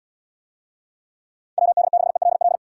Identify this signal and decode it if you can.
Morse code, heard as a keyed tone, about 70 wpm, 700 Hz. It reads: WD6ELL